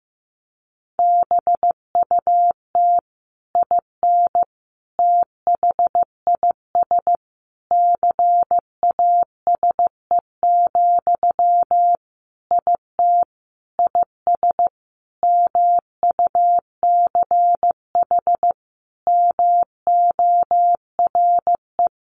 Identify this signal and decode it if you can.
Morse code, heard as a keyed tone, about 15 wpm, 700 Hz. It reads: BUT IN THIS CASE, IT IS MUCH MORE
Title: Sentences from Top 1200 Words 15wpm